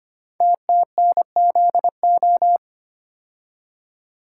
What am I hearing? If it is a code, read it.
Morse code, heard as a keyed tone, about 25 wpm, 700 Hz. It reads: TTNZO
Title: Code Groups - 5 Letters - 25wpm